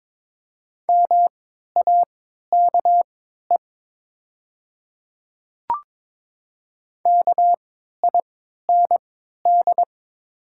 Morse code, heard as a keyed tone, about 22 wpm, spaced as for 13 wpm, 700 Hz. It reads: MAKE KIND
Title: 3x Character Spacing - Top 200 Words - 22wpm